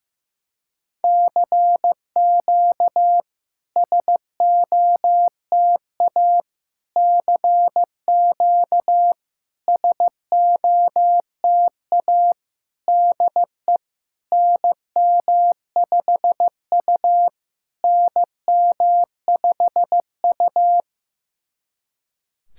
Morse code, heard as a keyed tone, about 15 wpm, 690 Hz. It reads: CQ SOTA CQ SOTA DE NM5U NM5U
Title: SOTA QSO Practice 15wpm